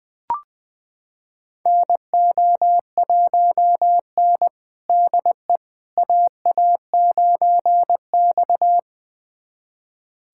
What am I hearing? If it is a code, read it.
Morse code, heard as a keyed tone, about 20 wpm, 700 Hz. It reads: NO1N DE AA9X
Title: USA Callsign DE Callsign - 4 Characters - 20wpm